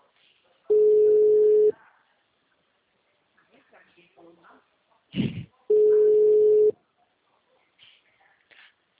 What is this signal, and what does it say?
Ringback tone (Europe, 425 Hz), with speech.
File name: out- -2012- - - .53586